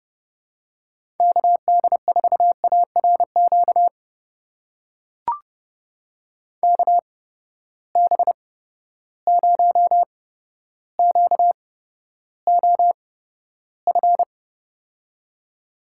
Morse code, heard as a keyed tone, about 30 wpm, spaced as for 13 wpm, 700 Hz. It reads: KD4ARQ KB0QOF
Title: USA Call Signs - 6 Characters - 8x Character Spacing - 30wpm